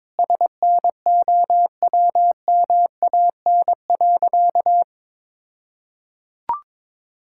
Morse code, heard as a keyed tone, about 22 wpm, 700 Hz. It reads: SNOWMAN.
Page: morse-code-ninja-practice-22wpm